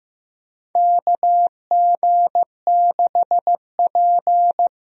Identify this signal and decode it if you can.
Morse code, heard as a keyed tone, about 15 wpm, 700 Hz. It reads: KG6P
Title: USA Call Signs - 4 Characters - Repeated 2x - 15wpm